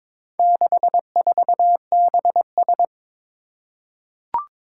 Morse code, heard as a keyed tone, about 22 wpm, 700 Hz. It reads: 64BS
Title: Four Challenging Letter-Number Combinations 22wpm